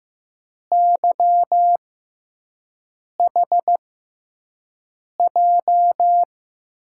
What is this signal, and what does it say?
Morse code, heard as a keyed tone, about 15 wpm, spaced as for 5 wpm, 700 Hz. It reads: YHJ
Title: USA Call Signs - 6 Characters - 6x Character Spacing - 15wpm